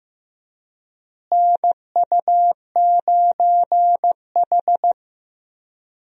Morse code, heard as a keyed tone, about 15 wpm, 700 Hz. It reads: NU9H
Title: USA Call Signs - 4 Characters - Repeated 3x - 15wpm